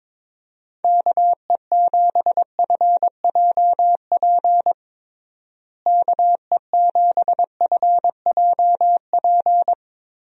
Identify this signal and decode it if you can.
Morse code, heard as a keyed tone, about 22 wpm, 700 Hz. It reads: KE7FJP KE7FJP
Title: USA Call Signs - 6 Characters - Repeated 3x - 22wpm